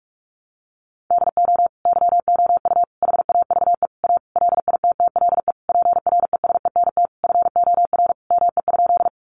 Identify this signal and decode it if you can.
Morse code, heard as a keyed tone, about 44 wpm, 700 Hz. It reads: DO YOU HAVE A LITTLE PRESENT FOR ME?
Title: Sentences from Top 800 Words 45wpm